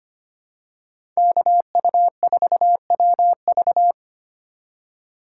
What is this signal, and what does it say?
Morse code, heard as a keyed tone, about 25 wpm, 700 Hz. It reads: KU4WV
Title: USA Call Signs - 5 Characters - 2x Character Spacing - 25wpm